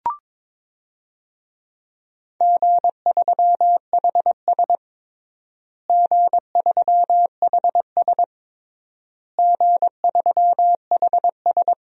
Morse code, heard as a keyed tone, about 22 wpm, 700 Hz. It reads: G3HS G3HS G3HS
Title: DXCC-100 Call Signs - 4 Characters - Repeated 3x - 22wpm